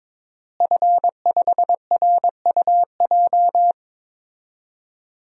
Morse code, heard as a keyed tone, about 21 wpm, 700 Hz. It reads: F5RUJ
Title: DXCC-100 Call Signs - 5 Characters - Repeated 3x - 22wpm